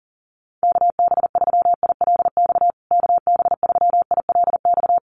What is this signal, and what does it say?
Morse code, heard as a keyed tone, about 40 wpm, 700 Hz. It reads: KB3ILX KB3ILX
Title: USA Call Signs Repeated 3x 40wpm